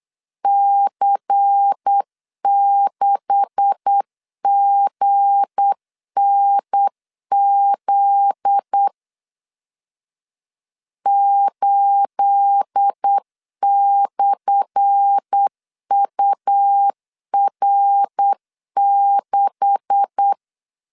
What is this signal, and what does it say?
Morse code, heard as a keyed tone, about 8 wpm, 800 Hz. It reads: C6GNZ 8/UR6
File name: LEZIONI / 10 SIGNS / 10A SLASH 8wpm 3452